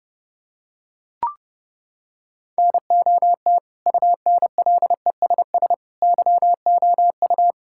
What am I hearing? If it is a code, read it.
Morse code, heard as a keyed tone, about 32 wpm, 700 Hz. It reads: NOT UNLESS YOU
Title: Sentences from Top 1000 Words - Review Part 6 - 30wpm